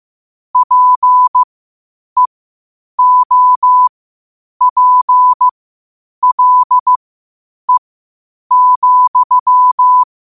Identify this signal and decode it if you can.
Morse code, heard as a keyed tone, about 15 wpm, spaced as for 8 wpm, 1000 Hz. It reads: PEOPLE,
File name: qotdfiles / 8wpm